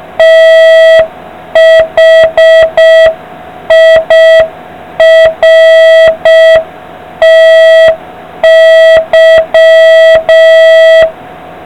Morse code, heard as a keyed tone, about 5 wpm, 650 Hz, with noise. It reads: THIRTY